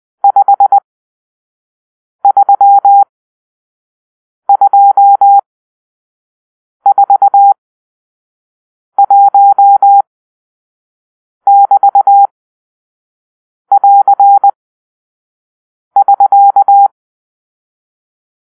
Morse code, heard as a keyed tone, about 20 wpm, spaced as for 5 wpm, 800 Hz. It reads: 53241<BT><AR><SK>
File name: arrl / Grp-6 Drill-2